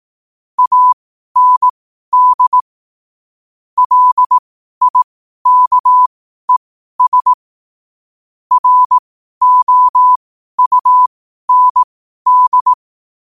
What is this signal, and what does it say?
Morse code, heard as a keyed tone, about 18 wpm, spaced as for 12 wpm, 1000 Hz. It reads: AND LIKES ROUND
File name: qotdfiles / 12wpm